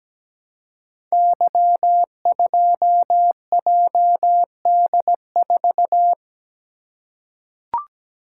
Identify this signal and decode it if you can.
Morse code, heard as a keyed tone, about 17 wpm, 700 Hz. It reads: Y2JD4
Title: Code Groups - 5 Letter:Numbers - 17wpm